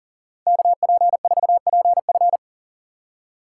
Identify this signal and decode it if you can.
Morse code, heard as a keyed tone, about 38 wpm, 700 Hz. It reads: KP4PF
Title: Speed Racing - USA 2x2 Call Signs 40wpm